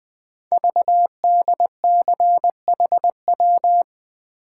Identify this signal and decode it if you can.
Morse code, heard as a keyed tone, about 20 wpm, 700 Hz. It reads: VDCHW
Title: Code Groups - 5 Letters - 20wpm